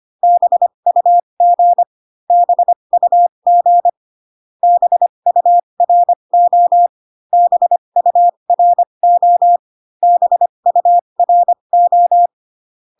Morse code, heard as a keyed tone, about 25 wpm, spaced as for 22 wpm, 700 Hz. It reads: BUG BUG BURO BURO BURO
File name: CW common abbreviations alphabetical repeated 3 times 25 18wpm